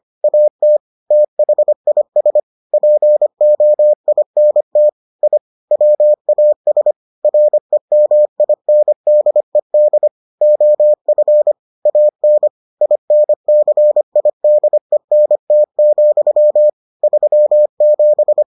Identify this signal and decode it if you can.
Morse code, heard as a keyed tone, about 25 wpm, 600 Hz. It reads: AT THIS POINT I WAS REMINDED OF AN INCIDENT, 37